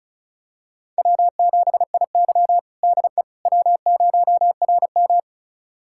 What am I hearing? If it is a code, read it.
Morse code, heard as a keyed tone, about 36 wpm, 700 Hz. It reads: W7IY DE W0RM
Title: USA Callsign DE Callsign - 4 Characters - 35wpm